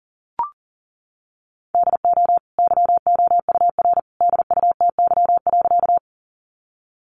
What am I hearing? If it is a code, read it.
Morse code, heard as a keyed tone, about 40 wpm, 700 Hz. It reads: DO YOUR DUTY.
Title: Sentences from Top 1500 Words 40wpm